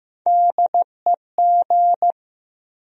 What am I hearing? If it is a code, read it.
Morse code, heard as a keyed tone, about 15 wpm, 700 Hz. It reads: DEG
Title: USA Call Sign Postfixes - 3 Letters 15wpm